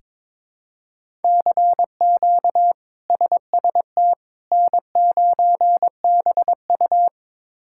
Morse code, heard as a keyed tone, about 22 wpm, 700 Hz. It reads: CQ SST N9BU
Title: K1USN SST QSO Practice - 22wpm